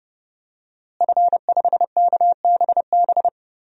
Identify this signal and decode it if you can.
Morse code, heard as a keyed tone, about 30 wpm, 700 Hz. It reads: F5KBB
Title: DXCC-100 Call Signs - 5 Characters - Repeated 3x - 30wpm